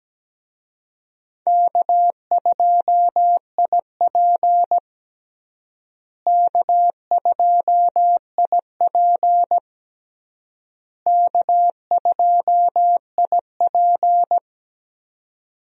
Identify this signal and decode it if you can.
Morse code, heard as a keyed tone, about 17 wpm, 700 Hz. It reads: K2IP K2IP K2IP
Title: USA Call Signs - 4 Characters - Repeated 3x - 17wpm